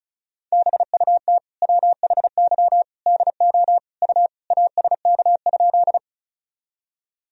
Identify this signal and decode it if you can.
Morse code, heard as a keyed tone, about 35 wpm, 700 Hz. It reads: BUT WHY DO U ASK?
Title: Sentences from Top 200 Words and Abbreviations 35wpm